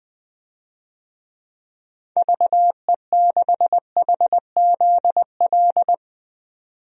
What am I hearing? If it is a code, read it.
Morse code, heard as a keyed tone, about 21 wpm, 700 Hz. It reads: VE6HZL
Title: DXCC-100 Call Signs - 6 Characters - 20wpm